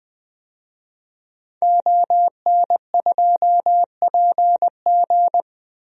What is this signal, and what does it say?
Morse code, heard as a keyed tone, about 21 wpm, 700 Hz. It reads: ON2PG
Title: DXCC-100 Call Signs - 5 Characters - Repeated 3x - 20wpm